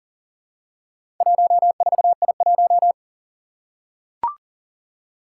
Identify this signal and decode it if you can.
Morse code, heard as a keyed tone, about 40 wpm, 700 Hz. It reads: 14I1
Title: Four Challenging Letter-Number Combinations 40wpm